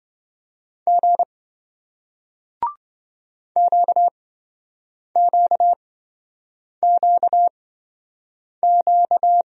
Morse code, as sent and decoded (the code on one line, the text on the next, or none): --. / --.- --.- --.- --.-
G QQQQ